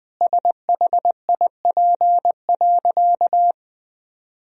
Morse code, heard as a keyed tone, about 19 wpm, 700 Hz. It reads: SHIP.
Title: Sentences from Top 1000 Words - Review Part 2 - 20wpm